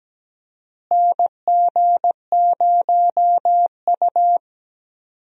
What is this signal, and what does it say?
Morse code, heard as a keyed tone, about 17 wpm, 700 Hz. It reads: NG0U
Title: USA Call Signs - 4 Characters - Repeated 3x - 17wpm